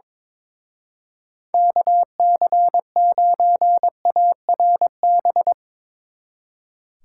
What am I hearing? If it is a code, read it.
Morse code, heard as a keyed tone, about 22 wpm, 700 Hz. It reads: KC9ARB
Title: Speed Racing - USA Call Signs 22wpm